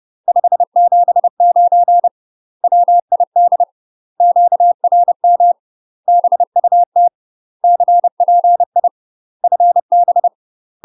Morse code, heard as a keyed tone, about 30 wpm, 700 Hz. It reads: 579 WID QRM BUT CPI FB
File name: UK MASTER QSOs TRAINING / 30WPM / UK MASTER QSOs 0026